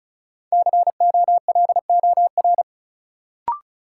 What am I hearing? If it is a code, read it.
Morse code, heard as a keyed tone, about 36 wpm, 700 Hz. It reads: COLOR
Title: Sentences from Top 1600 Words - New Word Drill Alt - 35wpm